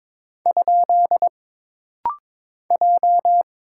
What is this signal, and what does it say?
Morse code, heard as a keyed tone, about 22 wpm, 700 Hz. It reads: ? J
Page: This rapid-fire timing is geared to push your ICR (Instant Character Recognition) skills.